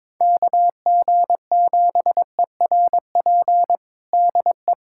Morse code, heard as a keyed tone, about 21 wpm, 700 Hz. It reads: KG7ERP DE
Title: USA Callsign DE Callsign 22wpm